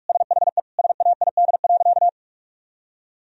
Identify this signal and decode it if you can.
Morse code, heard as a keyed tone, about 47 wpm, 700 Hz. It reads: SHE SAID.
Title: Sentences from Top 1000 Words - Review Part 3 - 45wpm